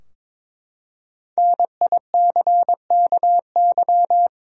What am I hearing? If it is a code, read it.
Morse code, heard as a keyed tone, about 22 wpm, 700 Hz. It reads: NICKY
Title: USA Common Names - Max 5 Letters - 22wpm